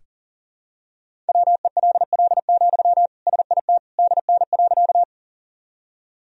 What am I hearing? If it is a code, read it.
Morse code, heard as a keyed tone, about 40 wpm, 700 Hz. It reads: WELL, SIT DN.
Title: Sentences from Top 500 Words and Abbreviations 40wpm